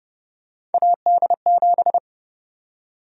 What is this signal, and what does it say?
Morse code, heard as a keyed tone, about 31 wpm, 700 Hz. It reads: AD7
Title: USA Call Sign Prefixes - 30wpm